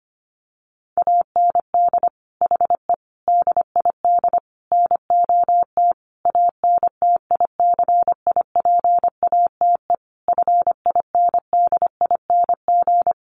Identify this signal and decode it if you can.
Morse code, heard as a keyed tone, about 25 wpm, 700 Hz. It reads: AND HE DID NOT ANTICIPATE FINDING
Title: Sentences from Top 1500 Words 25wpm